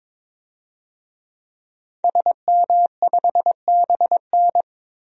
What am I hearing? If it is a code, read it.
Morse code, heard as a keyed tone, about 22 wpm, 700 Hz. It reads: SM5BN
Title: DXCC-100 Call Signs - 5 Characters - 22wpm